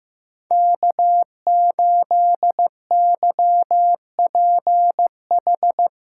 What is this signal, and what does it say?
Morse code, heard as a keyed tone, about 15 wpm, 700 Hz. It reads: K8YPH
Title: USA Callsign DE Callsign - 5 Characters - 15wpm